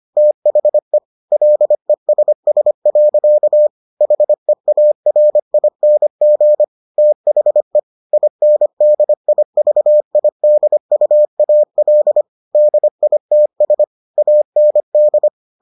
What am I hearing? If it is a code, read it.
Morse code, heard as a keyed tone, about 25 wpm, 600 Hz. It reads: THE LESS. HEARING THE INDIVIDUAL DITS AND